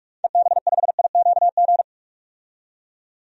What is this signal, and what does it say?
Morse code, heard as a keyed tone, about 46 wpm, 700 Hz. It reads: EB5IQG
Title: DXCC-100 Call Signs - 6 Characters - 45wpm